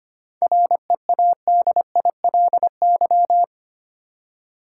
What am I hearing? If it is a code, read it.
Morse code, heard as a keyed tone, about 25 wpm, 700 Hz. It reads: READILY
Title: Seven Letter Words - 25wpm